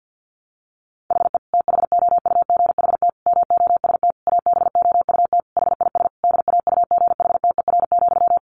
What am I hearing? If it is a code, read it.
Morse code, heard as a keyed tone, about 50 wpm, 700 Hz. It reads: HE THOUGHT MOST ABOUT HIS DAUGHTER,
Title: Sentences from Top 1500 Words 50wpm